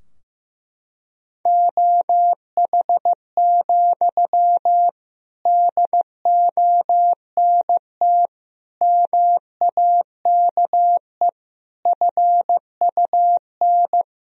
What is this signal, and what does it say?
Morse code, heard as a keyed tone, about 16 wpm, 700 Hz. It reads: OH, DONT MAKE FUN
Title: Sentences from Top 1000 Words - Review Part 4 - 15wpm